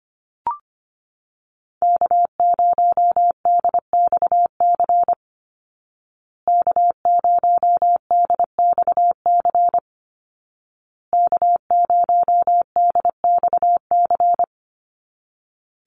Morse code, heard as a keyed tone, about 25 wpm, 700 Hz. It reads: K0DXC K0DXC K0DXC
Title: Call Signs Repeated 2x - 25wpm